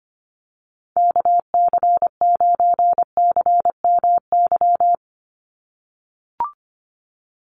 Morse code, heard as a keyed tone, about 25 wpm, 700 Hz. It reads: KC9CMY
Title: USA Call Signs 25wpm